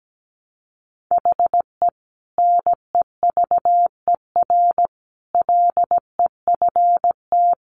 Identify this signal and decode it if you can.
Morse code, heard as a keyed tone, about 17 wpm, 700 Hz. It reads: HE NEVER LEFT
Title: Sentences from Top 1100 Words 17wpm